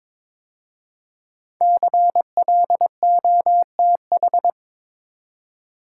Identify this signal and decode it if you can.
Morse code, heard as a keyed tone, about 22 wpm, 700 Hz. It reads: CLOTH
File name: 5 Letter Words - First 3 Letters Given - 22wpm